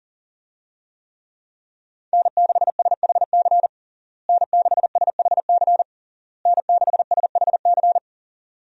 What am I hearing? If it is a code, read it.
Morse code, heard as a keyed tone, about 40 wpm, 700 Hz. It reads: N6SHC N6SHC N6SHC
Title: USA Call Signs - 5 Characters - Repeated 3x - 40wpm